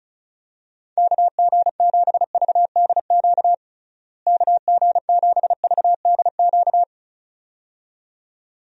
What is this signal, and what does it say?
Morse code, heard as a keyed tone, about 36 wpm, 700 Hz. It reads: KG7VDQ KG7VDQ
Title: USA Call Signs - 6 Characters - Repeated 2x - 35wpm